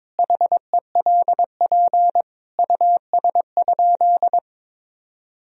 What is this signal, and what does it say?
Morse code, heard as a keyed tone, about 22 wpm, 700 Hz. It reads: HELP US?
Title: Sentences from Top 1000 Words - Review Part 3 - 22wpm